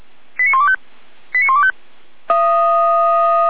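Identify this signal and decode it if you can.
Five-tone selective call (ZVEI), heard as quick runs of five tones, tones of about 70 ms. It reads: 89125 89125
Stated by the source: ZVEI five-tone selective call received on an SDR.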